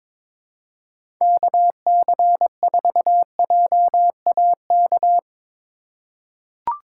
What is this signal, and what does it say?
Morse code, heard as a keyed tone, about 21 wpm, 700 Hz. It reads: KC4JAK
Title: USA Call Signs - 6 Characters - 4x Character Spacing - 22wpm